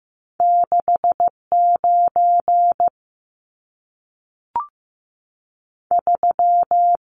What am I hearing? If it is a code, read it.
Morse code, heard as a keyed tone, about 15 wpm, spaced as for 3 wpm, 700 Hz. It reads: 6 93
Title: Numbers - 2-Digit 15wpm